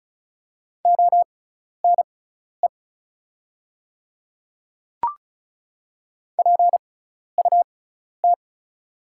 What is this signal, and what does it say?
Morse code, heard as a keyed tone, about 34 wpm, spaced as for 12 wpm, 690 Hz. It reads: ONE PUT
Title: 6x Character Spacing - Top 200 Words - 35wpm